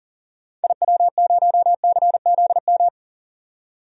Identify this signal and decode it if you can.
Morse code, heard as a keyed tone, about 40 wpm, 700 Hz. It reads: IW0CZM
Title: DXCC-100 Call Signs - 6 Characters - 40wpm